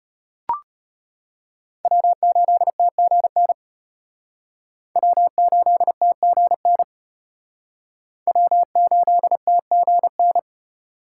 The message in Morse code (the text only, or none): W8TGN W8TGN W8TGN